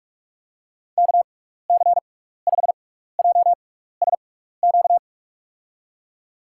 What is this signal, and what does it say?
Morse code, heard as a keyed tone, about 47 wpm, spaced as for 16 wpm, 700 Hz. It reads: KC5JSQ
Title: USA Call Signs - 6 Characters - 6x Character Spacing - 45wpm